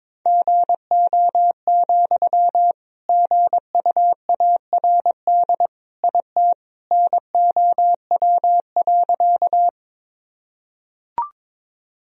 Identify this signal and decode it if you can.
Morse code, heard as a keyed tone, about 22 wpm, 700 Hz. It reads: GO, GUARD IT NOW.
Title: Sentences from Top 1000 Words - Review Part 4 - 22wpm